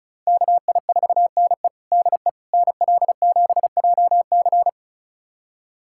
Morse code, heard as a keyed tone, about 36 wpm, 700 Hz. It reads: KI4NE DE NL7JC
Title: USA Callsign DE Callsign 35wpm